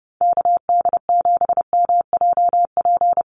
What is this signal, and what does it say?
Morse code, heard as a keyed tone, about 30 wpm, 700 Hz. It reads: KD7MJP
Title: USA Call Signs Repeated 3x 30wpm